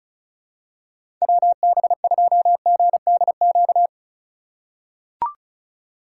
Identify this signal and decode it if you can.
Morse code, heard as a keyed tone, about 36 wpm, 700 Hz. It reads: WB2GDQ
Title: USA Call Signs - 6 Characters - 4x Character Spacing - 35wpm